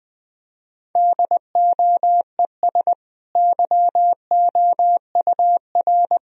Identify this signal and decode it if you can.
Morse code, heard as a keyed tone, about 20 wpm, 700 Hz. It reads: DOES YOUR
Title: Sentences from Top 1000 Words - Review Part 3 - 20wpm